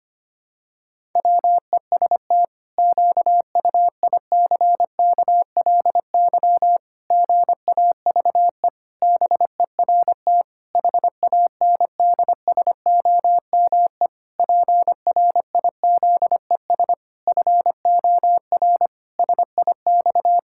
Morse code, heard as a keyed tone, about 25 wpm, 700 Hz. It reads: WEST QUICKLY GAVE BERT HANDSOME PRIZES FOR SIX
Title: Pangram Challenge v2 - 25wpm